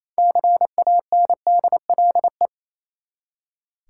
Morse code, heard as a keyed tone, about 27 wpm, 700 Hz. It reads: CANDLE
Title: Six Letter Words - 28wpm